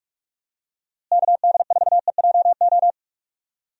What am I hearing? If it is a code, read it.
Morse code, heard as a keyed tone, about 47 wpm, 700 Hz. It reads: KD4EJO
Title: USA Call Signs - 6 Characters - Repeated 3x - 45wpm